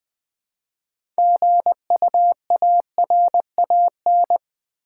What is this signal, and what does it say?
Morse code, heard as a keyed tone, about 20 wpm, 700 Hz. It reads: GUARAN
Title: Build A Word - 3 Letters At A Time - 9-Letter Words - 20wpm